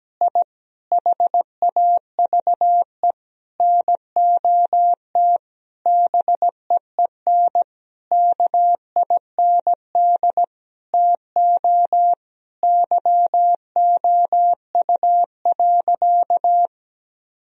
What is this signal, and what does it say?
Morse code, heard as a keyed tone, about 17 wpm, 700 Hz. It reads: I HAVE NOT BEEN KIND TO YOU.
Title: Sentences from Top 1000 Words - Review Part 2 - 17wpm